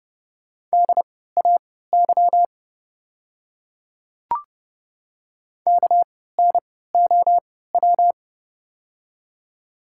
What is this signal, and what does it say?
Morse code, heard as a keyed tone, about 30 wpm, spaced as for 17 wpm, 700 Hz. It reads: DAY KNOW
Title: 3x Character Spacing - Top 100 Words - 30wpm